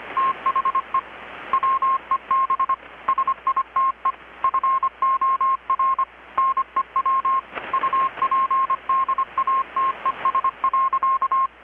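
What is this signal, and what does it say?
Morse code, heard as a keyed tone, about 25 wpm, 1070 Hz. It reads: THE WEB SITE FOR NEW UPDATES.
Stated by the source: Morse from KPH's Night of Nights XX maritime radiotelegraphy event, received in Chiapas, Mexico about 3,800 km away on a Kenwood R-600 with a grounded 10 m long-wire antenna.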